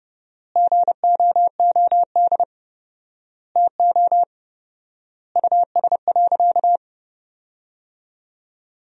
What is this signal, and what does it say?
Morse code, heard as a keyed tone, about 32 wpm, 700 Hz. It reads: GOOD TO US.
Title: 4x Word-Spacing - Sentences from Top 200 Words - 30wpm